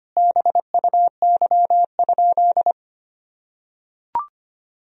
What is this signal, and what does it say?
Morse code, heard as a keyed tone, about 25 wpm, 700 Hz. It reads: BUY?
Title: Sentences from Top 1000 Words - Review Part 4 - 25wpm